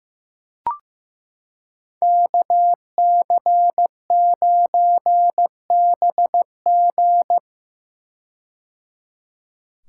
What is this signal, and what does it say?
Morse code, heard as a keyed tone, about 15 wpm, 700 Hz. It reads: KC9BG
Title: K1USN SST QSO Practice - 15wpm